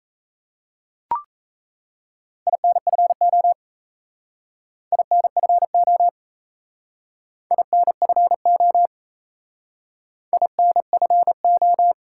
Morse code, heard as a keyed tone, about 36 wpm, 700 Hz. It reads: INFO INFO INFO INFO